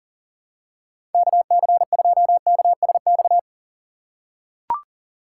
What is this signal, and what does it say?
Morse code, heard as a keyed tone, about 38 wpm, 700 Hz. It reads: KC2KSX
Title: USA Call Signs - 8x Character Spacing - 40wpm